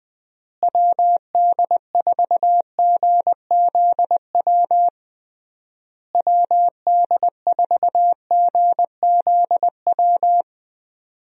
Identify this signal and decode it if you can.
Morse code, heard as a keyed tone, about 20 wpm, 700 Hz. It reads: WD4GZW WD4GZW
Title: USA Call Signs - 6 Characters - Repeated 3x - 20wpm